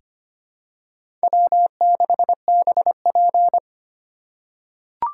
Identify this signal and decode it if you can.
Morse code, heard as a keyed tone, about 25 wpm, 700 Hz. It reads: W6BP
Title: USA Call Signs - 3x Character Spacing - 25wpm